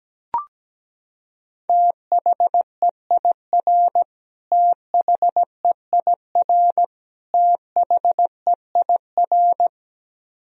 Morse code, heard as a keyed tone, about 17 wpm, 700 Hz. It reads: THEIR THEIR THEIR
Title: Top 100 Words Repeated 17wpm